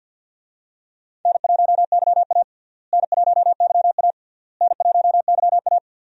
Morse code, heard as a keyed tone, about 47 wpm, 700 Hz. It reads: N1YA N1YA N1YA
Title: USA Call Signs - 4 Characters - Repeated 3x - 50wpm